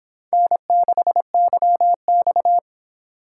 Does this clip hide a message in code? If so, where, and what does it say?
Yes: Morse code from 0.3 to 2.6 s, heard as a keyed tone, about 24 wpm, 700 Hz. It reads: N6YX